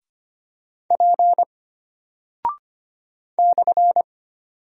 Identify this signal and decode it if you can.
Morse code, heard as a keyed tone, about 25 wpm, 700 Hz. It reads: P /
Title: Single Character (Letter:Number:Punctuation:Prosign) Alt 25wpm